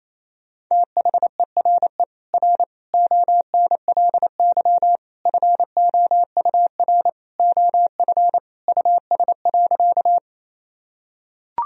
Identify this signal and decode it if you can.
Morse code, heard as a keyed tone, about 28 wpm, 700 Hz. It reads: THERE R ONLY FOUR OF US.
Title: Sentences from Top 300 Words and Abbreviations 28wpm